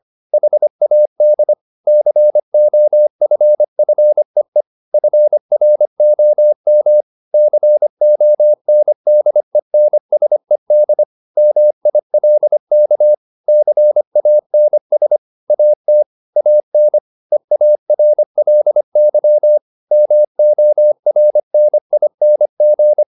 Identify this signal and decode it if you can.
Morse code, heard as a keyed tone, about 25 wpm, 600 Hz. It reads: HAD COFFEE FROM CONDENSED MILK CANS AT AN EARLY MORNING
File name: the old man and the sea part 1 25wpm